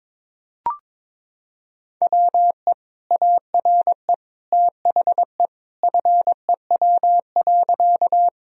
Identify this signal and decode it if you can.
Morse code, heard as a keyed tone, about 22 wpm, 700 Hz. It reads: WE ARE THE FEW.
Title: Sentences from Top 1000 Words - Review Part 1 - 22wpm